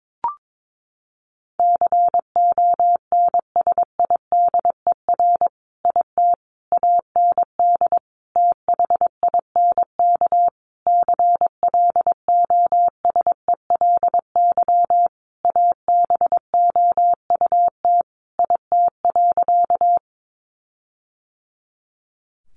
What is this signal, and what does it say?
Morse code, heard as a keyed tone, about 22 wpm, 700 Hz. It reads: CONSIDER IT AND THINK CLOSELY ABOUT IT.
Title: Sentences from Top 1400 Words 22wpm